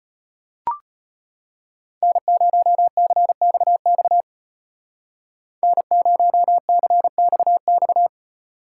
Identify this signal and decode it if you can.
Morse code, heard as a keyed tone, about 32 wpm, 700 Hz. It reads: N0CXX N0CXX